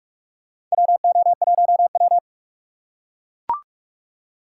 Morse code, heard as a keyed tone, about 44 wpm, 700 Hz. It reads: WO1W
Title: USA Call Signs - 4 Characters - 2x Character Spacing - 45wpm